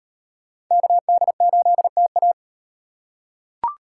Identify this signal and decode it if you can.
Morse code, heard as a keyed tone, about 38 wpm, 700 Hz. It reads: KD8TA